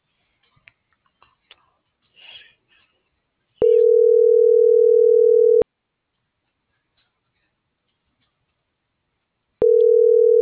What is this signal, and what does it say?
Ringback tone (North America), 440 + 480 Hz.